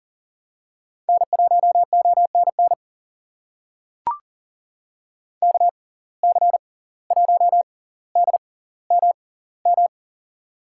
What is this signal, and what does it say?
Morse code, heard as a keyed tone, about 40 wpm, spaced as for 19 wpm, 700 Hz. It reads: N1ONN KC1DMM
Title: USA Call Signs - 6x Character Spacing - 40wpm